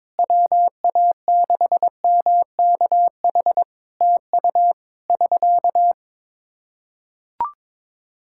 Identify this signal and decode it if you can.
Morse code, heard as a keyed tone, about 22 wpm, 700 Hz. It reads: WA6MKH TU <SK>
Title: K1USN SST QSO Practice 22wpm